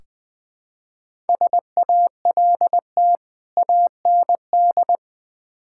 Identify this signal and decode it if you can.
Morse code, heard as a keyed tone, about 20 wpm, 700 Hz. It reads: SALT AND
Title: Build A Phrase - 5-Word Phrases - 20wpm